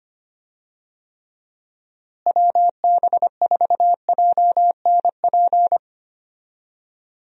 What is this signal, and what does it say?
Morse code, heard as a keyed tone, about 25 wpm, 700 Hz. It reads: WB4JNP
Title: USA Call Signs - 6 Characters - Repeated 3x - 25wpm